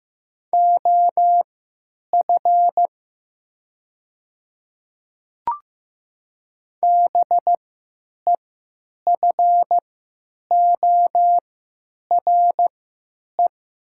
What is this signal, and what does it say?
Morse code, heard as a keyed tone, about 15 wpm, spaced as for 9 wpm, 700 Hz. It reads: OF BEFORE
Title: 3x Character Spacing - Top 200 Words - 15wpm